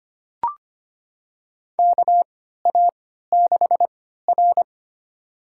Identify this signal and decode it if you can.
Morse code, heard as a keyed tone, about 25 wpm, spaced as for 14 wpm, 700 Hz. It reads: KA6R
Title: USA Call Signs - 4 Characters - 3x Character Spacing - 25wpm